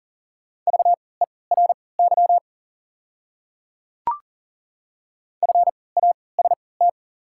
Morse code, heard as a keyed tone, about 40 wpm, spaced as for 23 wpm, 700 Hz. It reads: VERY FAST
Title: 3x Character Spacing - Top 500 Words - 40wpm